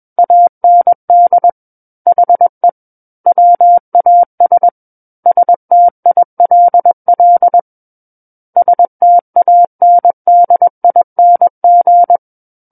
Morse code, heard as a keyed tone, about 21 wpm, 700 Hz. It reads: AND HE WAS STILL STANDING
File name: Book One / 12-Chapter 12 - 21 WPM